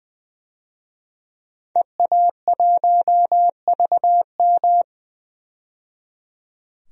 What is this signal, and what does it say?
Morse code, heard as a keyed tone, about 20 wpm, 700 Hz. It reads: EA1VM
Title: DXCC-100 Call Signs - 5 Characters - 20wpm